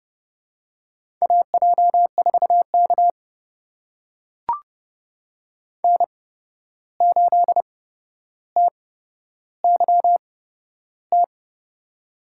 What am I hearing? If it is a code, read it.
Morse code, heard as a keyed tone, about 32 wpm, spaced as for 12 wpm, 700 Hz. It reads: AJ4K N8TYT
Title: USA Call Signs - 8x Character Spacing - 30wpm